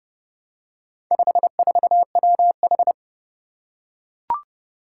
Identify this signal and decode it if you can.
Morse code, heard as a keyed tone, about 30 wpm, 700 Hz. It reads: 54WH
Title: Four Challenging Letter-Number Combinations 30wpm